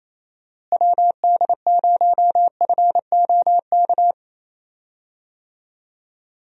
Morse code, heard as a keyed tone, about 28 wpm, 700 Hz. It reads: WD0FOK
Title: USA Call Signs - 6 Characters - Repeated 3x - 28wpm